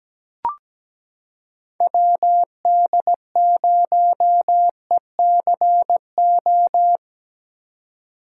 Morse code, heard as a keyed tone, about 17 wpm, 700 Hz. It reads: WD0ECO
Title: K1USN SST QSO Practice - 17wpm